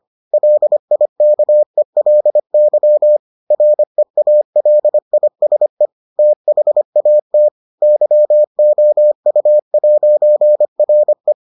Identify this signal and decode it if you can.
Morse code, heard as a keyed tone, about 25 wpm, 600 Hz. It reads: LIKELY REALISE THAT YOU'RE